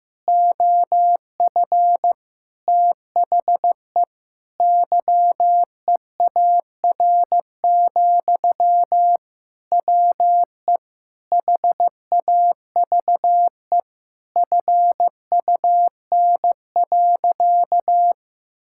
Morse code, heard as a keyed tone, about 15 wpm, 700 Hz. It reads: OF THE YEAR, WE HAVE FUN.
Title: Sentences from Top 1000 Words - Review Part 5 - 15wpm